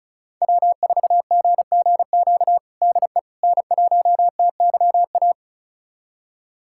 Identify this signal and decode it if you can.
Morse code, heard as a keyed tone, about 36 wpm, 700 Hz. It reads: W4GGQ DE N1TYA
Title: USA Callsign DE Callsign - 5 Characters - 35wpm